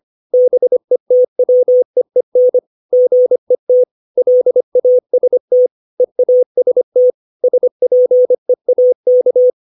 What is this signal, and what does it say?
Morse code, heard as a keyed tone, about 25 wpm, 500 Hz. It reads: BETWEEN GET LAST EAST SPEAK
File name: 632 words / 632 Words 25WPM